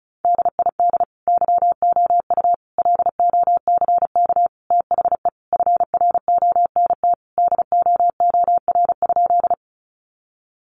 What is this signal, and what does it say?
Morse code, heard as a keyed tone, about 34 wpm, 700 Hz. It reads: DID YOU LOCK THE FRONT DOOR?
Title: Sentences from Top 1300 Words 35wpm